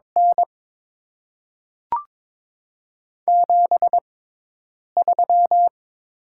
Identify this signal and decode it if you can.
Morse code, heard as a keyed tone, about 23 wpm, 690 Hz. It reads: N 7 3
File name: Top 20 QSO Elements - 6x Character Spacing - 22wpm